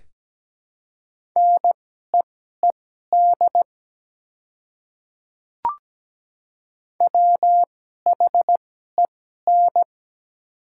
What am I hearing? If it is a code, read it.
Morse code, heard as a keyed tone, about 17 wpm, spaced as for 12 wpm, 690 Hz. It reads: NEED WHEN
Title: 2x Character Spacing - Top 300 Words - 17wpm